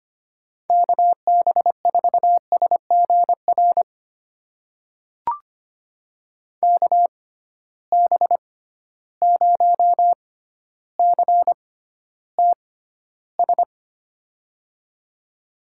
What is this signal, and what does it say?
Morse code, heard as a keyed tone, about 25 wpm, spaced as for 13 wpm, 700 Hz. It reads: KB4SGR KB0CTS
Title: USA Call Signs - 6 Characters - 6x Character Spacing - 25wpm